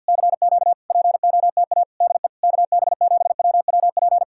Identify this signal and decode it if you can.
Morse code, heard as a keyed tone, about 66 wpm, spaced as for 58 wpm, 700 Hz. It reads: CQ POTA DE KB7WWP
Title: Speed-Racing Common QSO Phrases Encore - 50wpm